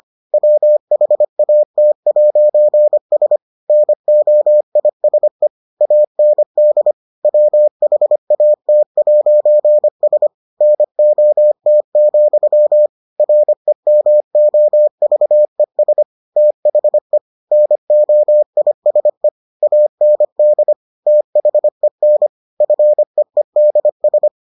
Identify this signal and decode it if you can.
Morse code, heard as a keyed tone, about 25 wpm, 600 Hz. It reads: WHAT'S NOISE AND WHAT'S NOT, REMOVES THE NOISE AND THEN FEEDS